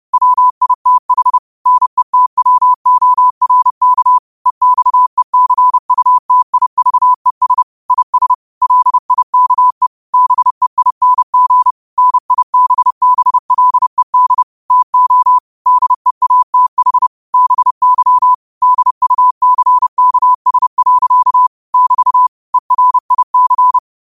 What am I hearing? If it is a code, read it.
Morse code, heard as a keyed tone, about 30 wpm, 1000 Hz. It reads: WITH NETWORK EXECUTIVES IS LIKE BEING NIBBLED TO DEATH BY DUCKS. <BT> ERIC
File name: qotdfiles / 30wpm